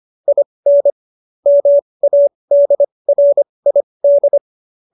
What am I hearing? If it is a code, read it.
Morse code, heard as a keyed tone, about 25 wpm, spaced as for 20 wpm, 570 Hz. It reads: IN MADRID
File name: qso215b 20